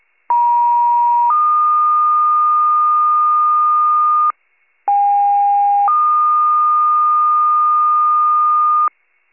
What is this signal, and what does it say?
Two-tone page, heard as a one-second tone, then a three-second tone, about 950 then 1220 Hz, about 810 then 1220 Hz, with noise. It is heard from 0.3 to 8.9 s.
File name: wcalert